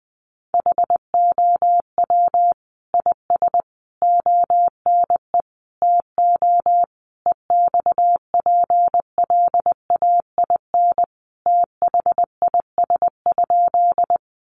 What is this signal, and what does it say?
Morse code, heard as a keyed tone, about 20 wpm, 700 Hz. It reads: HOW IS ONE TO EXPLAIN THIS?
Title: Sentences from Top 1100 Words 20wpm